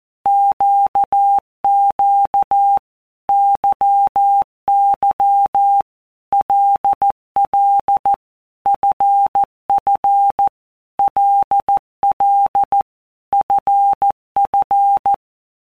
Morse code, heard as a keyed tone, about 14 wpm, 800 Hz. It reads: QQ YY LL FF LL FF
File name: Basic Letter Groups By WPM's / Groups@13 / LFCYQ@13